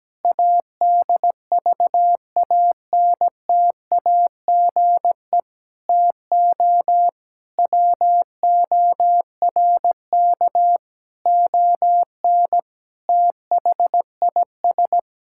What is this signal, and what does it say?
Morse code, heard as a keyed tone, about 17 wpm, 700 Hz. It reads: ADVANTAGE TO WORK ON THIS
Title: Sentences from Top 1000 Words - Review Part 6 - 17wpm